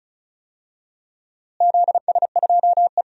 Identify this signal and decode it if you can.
Morse code, heard as a keyed tone, about 34 wpm, 700 Hz. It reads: ZS2E
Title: DXCC-100 Call Signs - 4 Characters - Repeated 2x - 35wpm